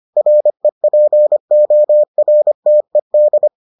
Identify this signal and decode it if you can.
Morse code, heard as a keyed tone, about 25 wpm, 600 Hz. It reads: REPORTED